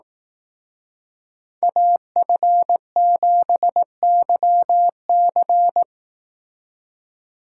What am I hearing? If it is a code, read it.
Morse code, heard as a keyed tone, about 18 wpm, 700 Hz. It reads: AF7YC
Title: Speed Racing - USA 2x2 Call Signs 15wpm